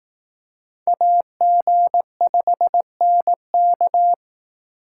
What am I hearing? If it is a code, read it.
Morse code, heard as a keyed tone, about 18 wpm, 700 Hz. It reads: AG5NK
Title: Speed Racing - USA 2x2 Call Signs 15wpm